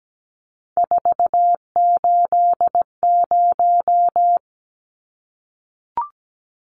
Morse code, heard as a keyed tone, about 17 wpm, 700 Hz. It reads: 480
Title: Numbers - 3-Digit 17wpm